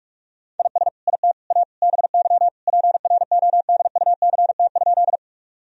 Morse code, heard as a keyed tone, about 46 wpm, 700 Hz. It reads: IS IT A BY PRODUCT?
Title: Sentences from Top 1000 Words - Review Part 2 - 45wpm